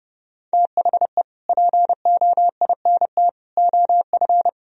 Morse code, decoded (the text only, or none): THE POINT OF